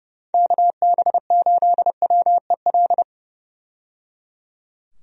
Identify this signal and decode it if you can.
Morse code, heard as a keyed tone, about 30 wpm, 700 Hz. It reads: KB8WEL
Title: USA Call Signs - 6 Characters - 30wpm